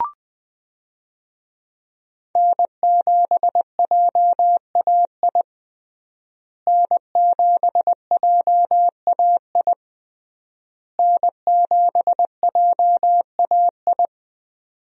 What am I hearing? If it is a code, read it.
Morse code, heard as a keyed tone, about 20 wpm, 700 Hz. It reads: N7JAI N7JAI N7JAI
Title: USA Call Signs - 5 Characters - Repeated 3x - 20wpm